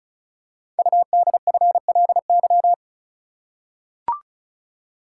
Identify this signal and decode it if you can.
Morse code, heard as a keyed tone, about 36 wpm, 700 Hz. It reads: UDFLY
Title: Code Groups - 5 Letters - 35wpm